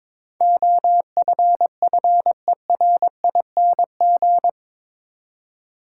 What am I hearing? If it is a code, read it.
Morse code, heard as a keyed tone, about 22 wpm, 700 Hz. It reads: OFFERING
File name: practice / 22wpm / Eight Letter Words - 22wpm